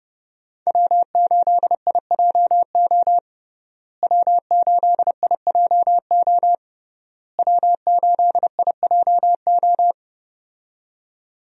Morse code, heard as a keyed tone, about 30 wpm, 700 Hz. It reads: W8IJO W8IJO W8IJO
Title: USA Call Signs - 5 Characters - Repeated 3x - 30wpm